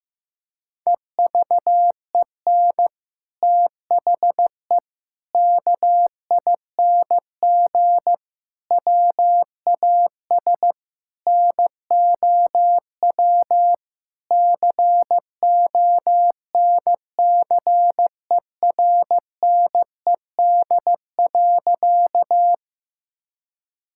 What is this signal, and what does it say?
Morse code, heard as a keyed tone, about 15 wpm, 700 Hz. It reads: EVEN THE KING WAS NOW CONCERNED.
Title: Sentences from Top 1000 Words - Review Part 6 - 15wpm